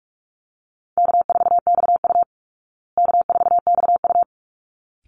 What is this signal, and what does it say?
Morse code, heard as a keyed tone, about 44 wpm, 700 Hz. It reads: K4XU K4XU
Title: Easy Call Signs Repeated 2x - 45wpm